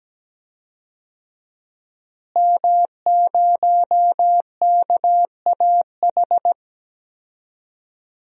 Morse code, heard as a keyed tone, about 17 wpm, 700 Hz. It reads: M0KAH
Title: DXCC-100 Call Signs - 5 Characters - Repeated 3x - 17wpm